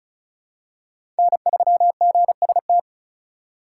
Morse code, heard as a keyed tone, about 34 wpm, 700 Hz. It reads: N3GST
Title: USA Call Signs - 5 Characters - 3x Character Spacing - 35wpm